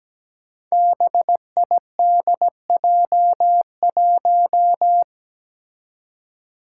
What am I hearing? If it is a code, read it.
Morse code, heard as a keyed tone, about 17 wpm, 700 Hz. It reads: BIDJ1
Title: Five Challenging Letter-Number Combinations 17wpm